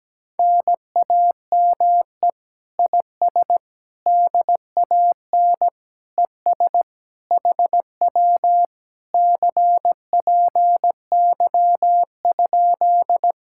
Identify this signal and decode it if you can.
Morse code, heard as a keyed tone, about 17 wpm, 700 Hz. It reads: NAME IS DAN ES HW CPY?
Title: Speed-Racing Common QSO Phrases - 17wpm